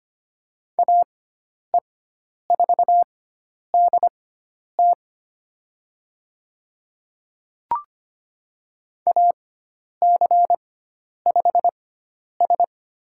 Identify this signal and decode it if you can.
Morse code, heard as a keyed tone, about 25 wpm, spaced as for 10 wpm, 700 Hz. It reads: AE4DT AC5S
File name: Lesson 16g - Letter c - Call Signs 25wpm